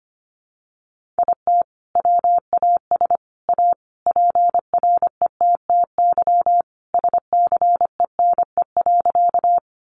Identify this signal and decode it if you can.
Morse code, heard as a keyed tone, about 25 wpm, 700 Hz. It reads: IT WAS A PRETTY SCENE.
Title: Sentences from Top 1100 Words 25wpm